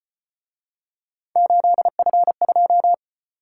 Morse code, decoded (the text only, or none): 8F2